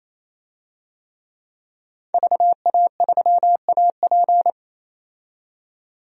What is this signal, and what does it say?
Morse code, heard as a keyed tone, about 28 wpm, 700 Hz. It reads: VA3AP
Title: DXCC-100 Call Signs - 5 Characters - 28wpm